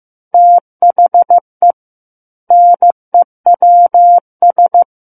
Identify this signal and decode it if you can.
Morse code, heard as a keyed tone, about 15 wpm, 700 Hz. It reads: THE NEWS
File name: Book One / 06-Chapter 6 - 15 WPM